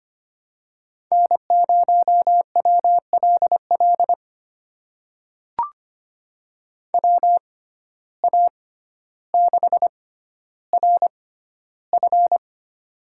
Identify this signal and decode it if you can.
Morse code, heard as a keyed tone, about 25 wpm, spaced as for 13 wpm, 700 Hz. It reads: N0WLL WA6RF
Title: USA Call Signs - 5 Characters - 6x Character Spacing - 25wpm